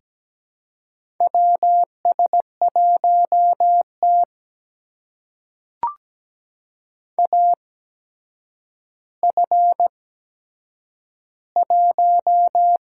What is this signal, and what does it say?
Morse code, heard as a keyed tone, about 17 wpm, spaced as for 8 wpm, 700 Hz. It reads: WS1T AF1
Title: USA Call Signs - 4 Characters - 8x Character Spacing - 17wpm